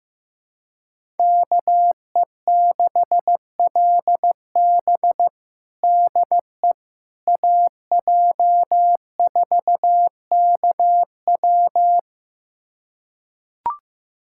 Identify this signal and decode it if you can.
Morse code, heard as a keyed tone, about 15 wpm, 700 Hz. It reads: KE6LB DE AJ4KW
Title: USA Callsign DE Callsign - 5 Characters - 15wpm